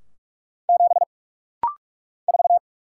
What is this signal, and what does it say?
Morse code, heard as a keyed tone, about 44 wpm, 700 Hz. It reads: 7 4
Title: Single Letter-Number - Rapid-Fire 45wpm